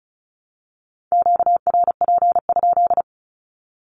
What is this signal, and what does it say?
Morse code, heard as a keyed tone, about 34 wpm, 700 Hz. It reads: QRP?
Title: Q-Codes 35wpm